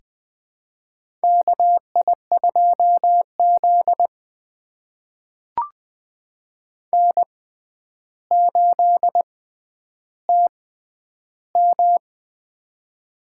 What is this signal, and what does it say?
Morse code, heard as a keyed tone, about 20 wpm, spaced as for 10 wpm, 700 Hz. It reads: KI2Z N8TM
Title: USA Call Signs - 4 Characters - 6x Character Spacing - 20wpm